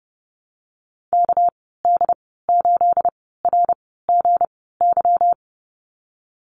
Morse code, heard as a keyed tone, about 32 wpm, spaced as for 17 wpm, 700 Hz. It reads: KD8RGY
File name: Lesson 37g - Letter x - Call Signs 30wpm